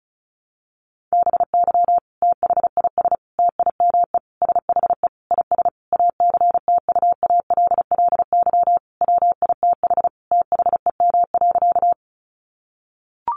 Morse code, heard as a keyed tone, about 34 wpm, 700 Hz. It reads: BY THIS TIME SHE IS ACTUALLY WITH THEM.
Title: Sentences from Top 700 Words 35wpm